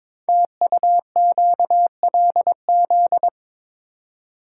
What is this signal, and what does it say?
Morse code, heard as a keyed tone, about 22 wpm, 700 Hz. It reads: TUQLZ
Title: Code Groups - 5 Letters - 22wpm